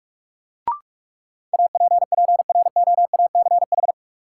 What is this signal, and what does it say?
Morse code, heard as a keyed tone, about 46 wpm, 700 Hz. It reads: APPROACH
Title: Sentences from Top 1400 Words - New Word Drill Alt - 45wpm